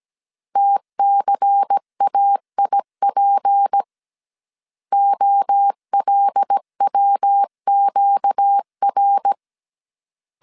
Morse code, heard as a keyed tone, about 17 wpm, 800 Hz. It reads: TCAIP OLWQR
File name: CW / DET / 1 9 1500C S A 16wpm 2828